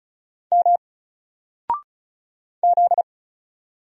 Morse code, heard as a keyed tone, about 34 wpm, 700 Hz. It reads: M Z
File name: Single-Letter-Number-35wpm